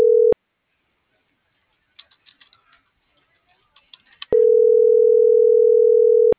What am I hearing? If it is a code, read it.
Ringback tone (North America), 440 + 480 Hz.